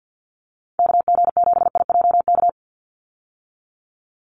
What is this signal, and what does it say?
Morse code, heard as a keyed tone, about 49 wpm, 700 Hz. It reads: KG7IJK
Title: USA Call Signs 50wpm